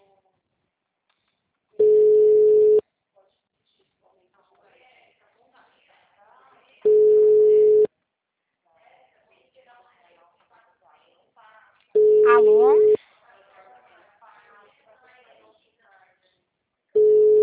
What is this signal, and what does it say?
Ringback tone (Europe, 425 Hz), with speech.